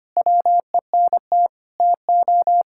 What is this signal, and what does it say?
Morse code, heard as a keyed tone, about 25 wpm, 700 Hz. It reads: WENT TO
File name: practice / 25wpm / Sentences with 4-Word Phrases - 25wpm